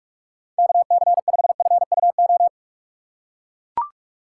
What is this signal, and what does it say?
Morse code, heard as a keyed tone, about 43 wpm, 700 Hz. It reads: KC5FUO
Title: USA Call Signs - 4x Character Spacing - 45wpm